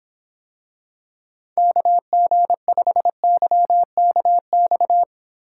Morse code, heard as a keyed tone, about 25 wpm, 700 Hz. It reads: KG5YKX